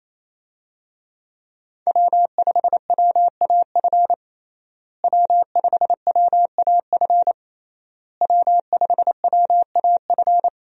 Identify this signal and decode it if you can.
Morse code, heard as a keyed tone, about 28 wpm, 700 Hz. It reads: W5WAF W5WAF W5WAF
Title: USA Call Signs - 5 Characters - Repeated 3x - 28wpm